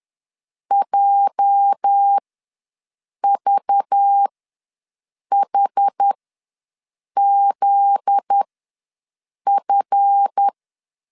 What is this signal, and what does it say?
Morse code, heard as a keyed tone, about 10 wpm, spaced as for 6 wpm, 800 Hz. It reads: JVHZF
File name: LEZIONI / 8 YJ45 / 8 YJ ws 10wpm 4043